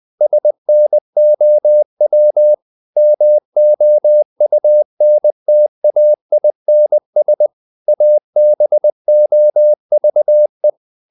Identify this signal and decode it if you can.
Morse code, heard as a keyed tone, about 20 wpm, 600 Hz. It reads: SNOW MOUNTAINS ABOVE
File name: the old man and the sea / the old man and the sea part 1 20wpm